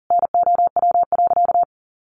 Morse code, heard as a keyed tone, about 40 wpm, 700 Hz. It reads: NOW.
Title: Sentences from Top 1400 Words 40wpm